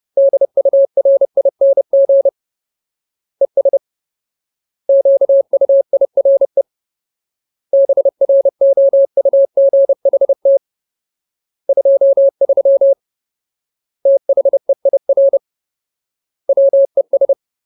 Morse code, heard as a keyed tone, about 30 wpm, 550 Hz. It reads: DURING ES QUIRE BROUGHT 23 THEIR WES
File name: Ultimate Word List / 30WPM Ultimate